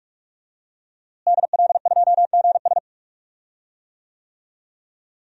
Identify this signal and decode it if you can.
Morse code, heard as a keyed tone, about 44 wpm, 700 Hz. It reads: DL2GS
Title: DXCC-100 Call Signs - 5 Characters - Repeated 3x - 45wpm